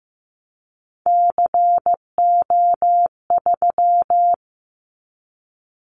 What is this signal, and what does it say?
Morse code, heard as a keyed tone, about 15 wpm, 700 Hz. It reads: CO3
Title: Callsign Prefixes for Top 100 DXCC - 15wpm